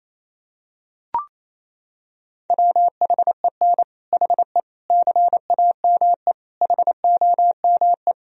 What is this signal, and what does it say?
Morse code, heard as a keyed tone, about 28 wpm, 700 Hz. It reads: WHEN HE CAME HOME
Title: Sentences from Top 1000 Words - Review Part 3 - 28wpm